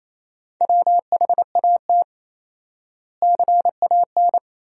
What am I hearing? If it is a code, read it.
Morse code, heard as a keyed tone, about 28 wpm, 700 Hz. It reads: WHAT CAN
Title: 4x Word-Spacing - Sentences from Top 100 Words - 28wpm